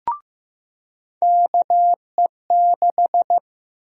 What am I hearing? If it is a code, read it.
Morse code, heard as a keyed tone, about 15 wpm, 700 Hz. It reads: KE6
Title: USA Call Sign Prefixes - 15wpm